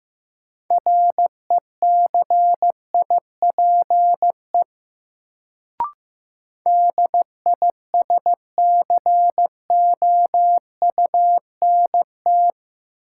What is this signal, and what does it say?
Morse code, heard as a keyed tone, about 15 wpm, 700 Hz. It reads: RECIPE DISCOUNT
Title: Sentences from Top 1200 Words - New Word Drill Alt - 15wpm